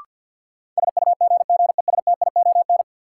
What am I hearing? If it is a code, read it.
Morse code, heard as a keyed tone, about 53 wpm, 700 Hz. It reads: SUGGESTION
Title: Sentences from Top 1600 Words - New Word Drill Alt - 50wpm